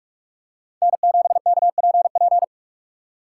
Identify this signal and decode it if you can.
Morse code, heard as a keyed tone, about 43 wpm, 690 Hz. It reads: N7KPP
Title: USA Call Signs - 4x Character Spacing - 45wpm